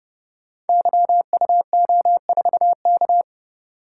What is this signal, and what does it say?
Morse code, heard as a keyed tone, about 30 wpm, 700 Hz. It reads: YUO4K
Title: Code Groups - 5 Letter:Numbers - 30wpm